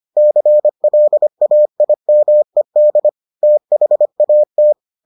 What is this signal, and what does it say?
Morse code, heard as a keyed tone, about 25 wpm, 600 Hz. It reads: CLAIMED THAT